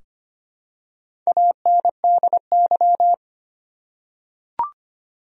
Morse code, heard as a keyed tone, about 25 wpm, 700 Hz. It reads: ANDY
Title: USA Common Names - Max 4 Letters - 25wpm